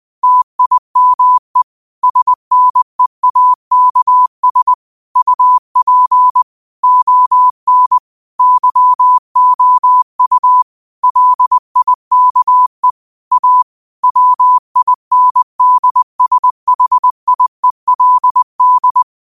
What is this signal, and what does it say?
Morse code, heard as a keyed tone, about 20 wpm, 1000 Hz. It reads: TIME SNEAKS UP ON YOU LIKE A WINDSHIELD